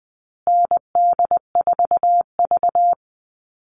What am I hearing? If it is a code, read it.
Morse code, heard as a keyed tone, about 20 wpm, 700 Hz. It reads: ND4V
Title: Call Signs Repeated 2x - 20wpm